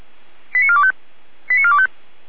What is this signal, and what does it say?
Five-tone selective call (ZVEI), heard as quick runs of five tones, tones of about 70 ms. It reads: 89425 89425